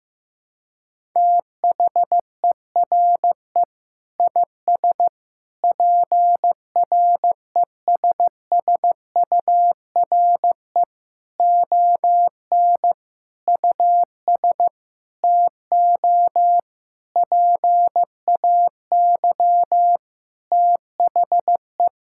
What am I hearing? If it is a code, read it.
Morse code, heard as a keyed tone, about 16 wpm, 700 Hz. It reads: THERE IS PRESSURE ON US TO PAY THE
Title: Sentences from Top 1000 Words - Review Part 6 - 15wpm